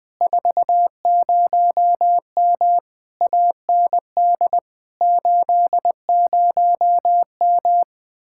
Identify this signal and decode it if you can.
Morse code, heard as a keyed tone, about 20 wpm, 700 Hz. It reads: 40M AND 80M
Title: Speed-Racing Common QSO Phrases - 20wpm